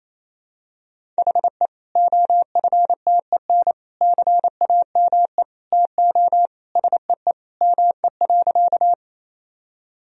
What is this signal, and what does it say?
Morse code, heard as a keyed tone, about 28 wpm, 700 Hz. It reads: HE OFTEN CAME TO SEE ME.
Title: Sentences from Top 1000 Words - Review Part 1 - 28wpm